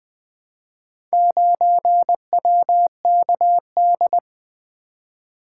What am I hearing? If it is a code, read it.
Morse code, heard as a keyed tone, about 20 wpm, 700 Hz. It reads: 9WKD
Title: Four Letter-Number Combinations - 20wpm